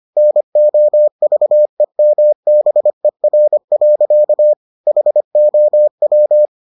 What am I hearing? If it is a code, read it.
Morse code, heard as a keyed tone, about 25 wpm, 600 Hz. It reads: NOVEMBER. HOW